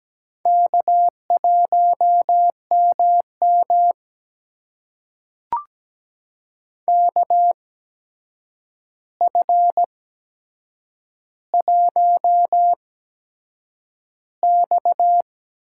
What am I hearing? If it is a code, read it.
Morse code, heard as a keyed tone, about 17 wpm, spaced as for 7 wpm, 700 Hz. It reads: K1MM KF1X
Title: USA Call Signs - 4 Characters - 8x Character Spacing - 17wpm